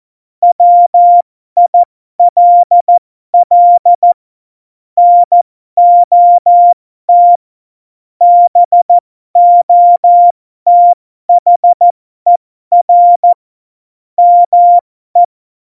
Morse code, heard as a keyed tone, about 14 wpm, spaced as for 12 wpm, 700 Hz. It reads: WILL NOT BOTHER ME
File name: Morsekurs / Witze